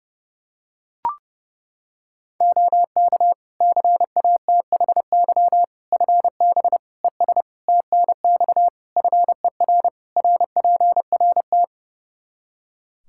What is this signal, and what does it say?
Morse code, heard as a keyed tone, about 30 wpm, 700 Hz. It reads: OK CATHY FB ES TNX FER RPRT
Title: Practice QSO - Exchange 2 - Info - 30wpm